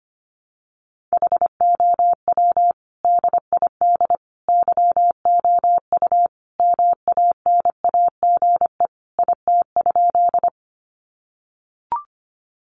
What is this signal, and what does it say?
Morse code, heard as a keyed tone, about 25 wpm, 700 Hz. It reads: HOW DID YOU MANAGE IT?
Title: Sentences from Top 1400 Words 25wpm